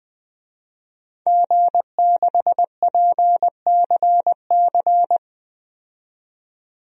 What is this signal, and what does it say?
Morse code, heard as a keyed tone, about 21 wpm, 700 Hz. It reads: G6PCC
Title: DXCC-100 Call Signs - 5 Characters - 20wpm